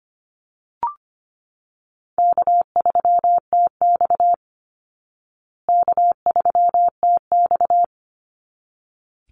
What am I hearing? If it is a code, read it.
Morse code, heard as a keyed tone, about 25 wpm, 700 Hz. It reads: K3TX K3TX
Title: Easy Call Signs Repeated 2x - 25wpm